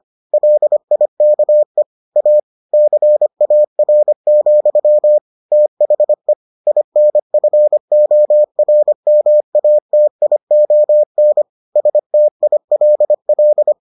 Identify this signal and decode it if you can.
Morse code, heard as a keyed tone, about 25 wpm, 600 Hz. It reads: LIKE A CAR, THE INFORMATION STILL